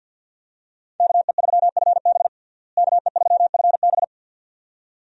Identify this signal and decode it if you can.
Morse code, heard as a keyed tone, about 51 wpm, 700 Hz. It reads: KE3FB KE3FB
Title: USA Call Signs - 5 Characters - Repeated 2x - 50wpm